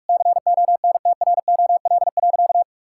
Morse code, heard as a keyed tone, about 46 wpm, 700 Hz. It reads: CONTROL.
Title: Sentences from Top 1000 Words - Review Part 5 - 45wpm